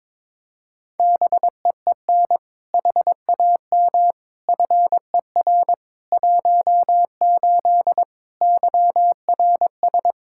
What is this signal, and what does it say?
Morse code, heard as a keyed tone, about 22 wpm, 700 Hz. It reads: BEEN HAM FER 18 YRS
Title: Practice QSO - Exchange 3 - More Info - 22wpm